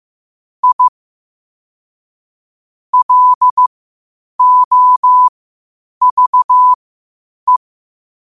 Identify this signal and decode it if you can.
Morse code, heard as a keyed tone, about 15 wpm, spaced as for 8 wpm, 1000 Hz. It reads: I LOVE
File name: qotdfiles / 8wpm